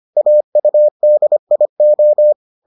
Morse code, heard as a keyed tone, about 25 wpm, 600 Hz. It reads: AUDIO